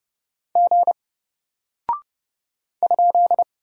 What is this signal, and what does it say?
Morse code, heard as a keyed tone, about 30 wpm, 700 Hz. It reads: G ?
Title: Single Character (Letter:Number:Punctuation:Prosign) Alt 30wpm